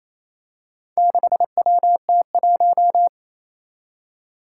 Morse code, heard as a keyed tone, about 28 wpm, 700 Hz. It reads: 6WT1
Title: Four Challenging Letter-Number Combinations 28wpm